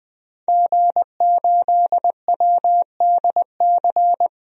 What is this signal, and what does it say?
Morse code, heard as a keyed tone, about 21 wpm, 700 Hz. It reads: G8WDC
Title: DXCC-100 Call Signs - 5 Characters - 20wpm